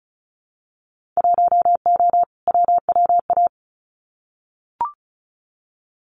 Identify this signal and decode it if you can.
Morse code, heard as a keyed tone, about 36 wpm, 700 Hz. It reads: 1O WWA
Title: Winter Field Day Exchanges 35wpm